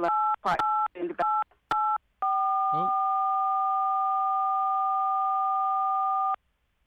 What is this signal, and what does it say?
DTMF, heard as telephone touch tones, tones of about 270 ms, with speech. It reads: 88884